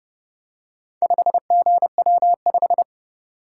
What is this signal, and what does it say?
Morse code, heard as a keyed tone, about 29 wpm, 700 Hz. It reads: 5GW5